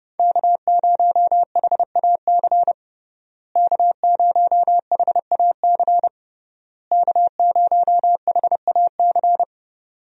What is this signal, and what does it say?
Morse code, heard as a keyed tone, about 30 wpm, 700 Hz. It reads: K0HAC K0HAC K0HAC
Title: USA Call Signs - 5 Characters - Repeated 3x - 30wpm